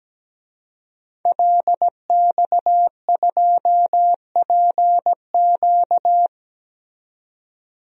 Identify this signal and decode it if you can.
Morse code, heard as a keyed tone, about 17 wpm, 700 Hz. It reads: LX2PQ
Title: DXCC-100 Call Signs - 5 Characters - 17wpm